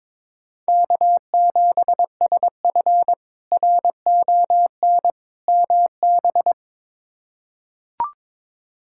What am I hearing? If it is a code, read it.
Morse code, heard as a keyed tone, about 22 wpm, 700 Hz. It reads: K7SF RON MB
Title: K1USN SST QSO Practice - 22wpm